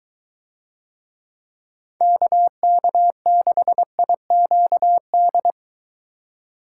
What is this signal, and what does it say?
Morse code, heard as a keyed tone, about 21 wpm, 700 Hz. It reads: KK6IQD